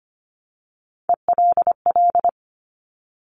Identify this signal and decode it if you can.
Morse code, heard as a keyed tone, about 25 wpm, 700 Hz. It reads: ELL
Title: Top Three Letter Combinations in Words 25wpm